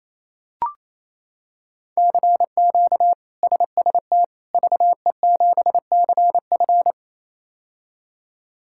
Morse code, heard as a keyed tone, about 28 wpm, 700 Hz. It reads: CQ SST VE7CF
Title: K1USN SST QSO Practice - 28wpm